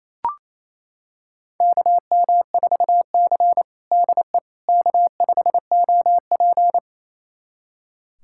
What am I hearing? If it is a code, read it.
Morse code, heard as a keyed tone, about 27 wpm, 700 Hz. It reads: KM4C DE K5OP
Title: USA Callsign DE Callsign - 4 Characters - 28wpm